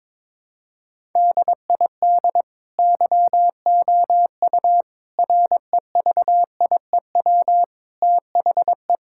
Morse code, heard as a keyed tone, about 22 wpm, 700 Hz. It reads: DID YOU REVIEW THE
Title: Sentences from Top 1000 Words - Review Part 6 - 22wpm